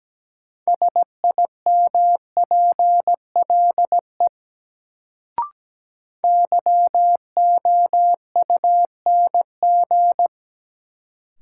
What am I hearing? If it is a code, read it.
Morse code, heard as a keyed tone, about 17 wpm, 700 Hz. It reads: SIMPLE YOUNG
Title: Sentences from Top 400 Words - New Word Drill - 17wpm